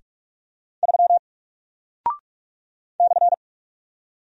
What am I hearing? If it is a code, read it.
Morse code, heard as a keyed tone, about 44 wpm, 700 Hz. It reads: 3 /
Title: Single Character (Letter:Number:Punctuation:Prosign) Alt 45wpm